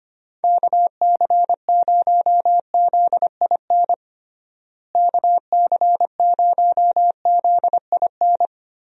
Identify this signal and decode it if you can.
Morse code, heard as a keyed tone, about 25 wpm, 700 Hz. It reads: KC0ZIN KC0ZIN
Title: USA Call Signs - 6 Characters - Repeated 3x - 25wpm